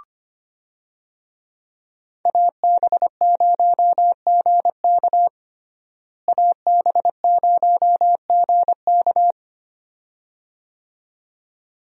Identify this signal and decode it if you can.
Morse code, heard as a keyed tone, about 25 wpm, 700 Hz. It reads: AB0GK AB0GK
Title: USA Call Signs - 5 Characters - Repeated 2x - 25wpm